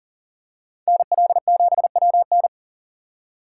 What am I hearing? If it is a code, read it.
Morse code, heard as a keyed tone, about 38 wpm, 700 Hz. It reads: NL7WN
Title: USA Call Signs - 5 Characters - 2x Character Spacing - 40wpm